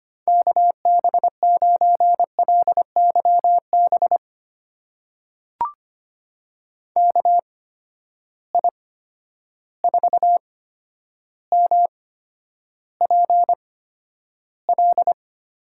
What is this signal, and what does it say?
Morse code, heard as a keyed tone, about 25 wpm, spaced as for 11 wpm, 700 Hz. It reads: KB9LYB KI4MPL
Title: USA Call Signs - 6 Characters - 8x Character Spacing - 25wpm